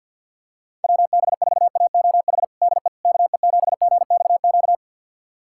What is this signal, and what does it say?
Morse code, heard as a keyed tone, about 53 wpm, 700 Hz. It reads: WB4AOH DE KE7GXQ
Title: USA Callsign DE Callsign 50wpm